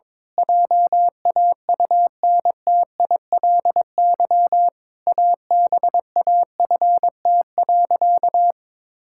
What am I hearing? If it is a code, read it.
Morse code, heard as a keyed tone, about 22 wpm, 700 Hz. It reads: JAUNTILY ABAFT.
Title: Pangram Challenge v2 - 22wpm